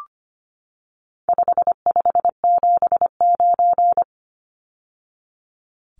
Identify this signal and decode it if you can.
Morse code, heard as a keyed tone, about 25 wpm, 700 Hz. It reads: 5579
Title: Numbers - 4-Digit 25wpm